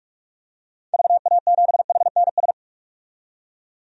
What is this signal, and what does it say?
Morse code, heard as a keyed tone, about 46 wpm, 700 Hz. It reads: VA7HNS
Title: DXCC-100 Call Signs - 6 Characters - 45wpm